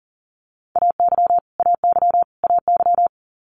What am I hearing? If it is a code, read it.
Morse code, heard as a keyed tone, about 40 wpm, 700 Hz. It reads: AY AY AY
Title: Multi-letter Phonemes Repeated 3x 40wpm